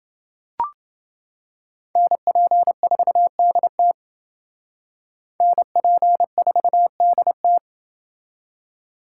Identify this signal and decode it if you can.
Morse code, heard as a keyed tone, about 28 wpm, 700 Hz. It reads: NP4DT NP4DT